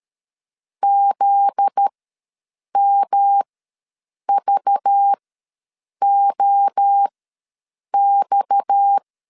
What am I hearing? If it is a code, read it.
Morse code, heard as a keyed tone, about 13 wpm, spaced as for 7 wpm, 800 Hz. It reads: ZMVOX
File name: LEZIONI / 9 28 ws 12wpm 3354